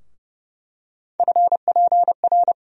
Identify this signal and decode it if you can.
Morse code, heard as a keyed tone, about 30 wpm, 700 Hz. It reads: FPR
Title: USA Call Sign Postfixes - 3 Letters 30wpm